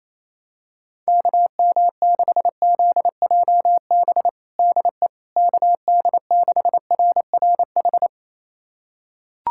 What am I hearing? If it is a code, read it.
Morse code, heard as a keyed tone, about 27 wpm, 700 Hz. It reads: KM6ZJB DE KD6RRH
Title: USA Callsign DE Callsign 28wpm